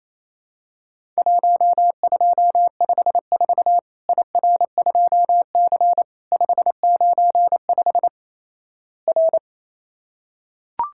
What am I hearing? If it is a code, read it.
Morse code, heard as a keyed tone, about 28 wpm, 700 Hz. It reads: 1254 IR2C 595 R
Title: Worked All Europe DX Contest - QTC Exchange - 28wpm